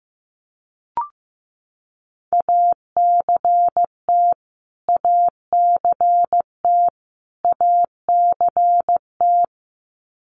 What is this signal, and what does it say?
Morse code, heard as a keyed tone, about 15 wpm, 700 Hz. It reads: ACT ACT ACT
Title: Top 500 Words Repeated 15wpm